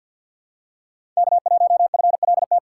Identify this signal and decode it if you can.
Morse code, heard as a keyed tone, about 47 wpm, 700 Hz. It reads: K1FLT
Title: USA Call Signs - 5 Characters - 3x Character Spacing - 50wpm